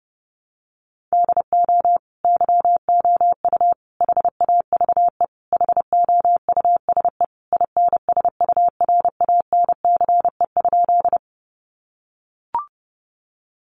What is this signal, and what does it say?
Morse code, heard as a keyed tone, about 30 wpm, 700 Hz. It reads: DO YOU HAVE HOUSE INSURANCE?
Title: Sentences from Top 1100 Words 30wpm